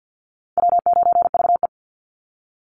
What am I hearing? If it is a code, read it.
Morse code, heard as a keyed tone, about 50 wpm, 700 Hz. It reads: W9VE
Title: Call Signs - 50wpm